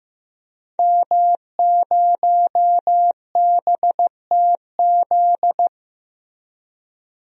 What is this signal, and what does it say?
Morse code, heard as a keyed tone, about 15 wpm, 700 Hz. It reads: M0BTZ
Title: DXCC-100 Call Signs - 5 Characters - Repeated 3x - 15wpm